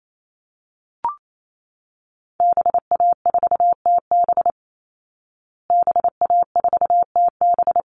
Morse code, heard as a keyed tone, about 28 wpm, 700 Hz. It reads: BA4TB BA4TB
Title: Call Signs Repeated 3x - 28wpm